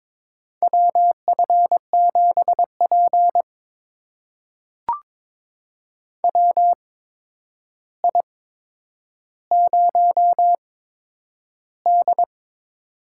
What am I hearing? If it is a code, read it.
Morse code, heard as a keyed tone, about 21 wpm, spaced as for 9 wpm, 700 Hz. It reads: WF7P WI0D
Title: USA Call Signs - 4 Characters - 8x Character Spacing - 22wpm